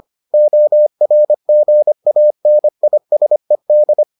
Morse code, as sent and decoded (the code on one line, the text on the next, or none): --- .-. --. .- -. .. ... -..
ORGANISED